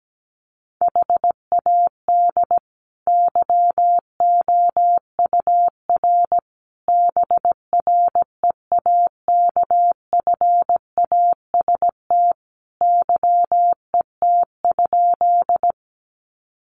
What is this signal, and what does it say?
Morse code, heard as a keyed tone, about 17 wpm, 700 Hz. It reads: HAD YOUR BREAKFAST YET?
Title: Sentences from Top 1500 Words 17wpm